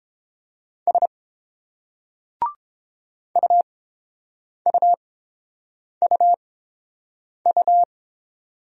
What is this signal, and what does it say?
Morse code, heard as a keyed tone, about 30 wpm, spaced as for 7 wpm, 700 Hz. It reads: S UUUU